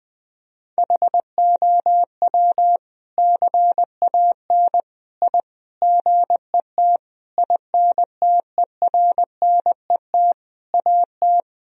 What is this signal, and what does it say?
Morse code, heard as a keyed tone, about 20 wpm, 700 Hz. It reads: HOW CAN I GET INTERNET AT
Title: Sentences from Top 1000 Words - Review Part 4 - 20wpm